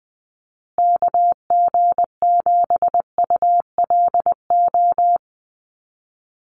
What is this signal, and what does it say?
Morse code, heard as a keyed tone, about 20 wpm, 700 Hz. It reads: KG7ULO
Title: USA Call Signs Repeated 3x 20wpm